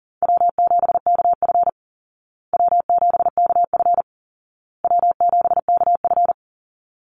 Morse code, heard as a keyed tone, about 40 wpm, 700 Hz. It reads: W7KF W7KF W7KF
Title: Easy Call Signs Repeated 3x - 40wpm